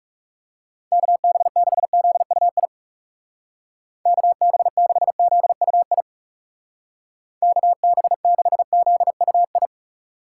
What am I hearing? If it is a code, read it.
Morse code, heard as a keyed tone, about 35 wpm, 700 Hz. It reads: KB6ZUI KB6ZUI KB6ZUI
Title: Speed Racing - USA Call Signs 30wpm